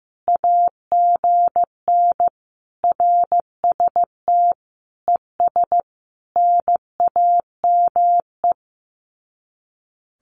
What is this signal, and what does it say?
Morse code, heard as a keyed tone, about 15 wpm, 700 Hz. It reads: AGN RST ES NAME
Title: Common QSO Phrases 15wpm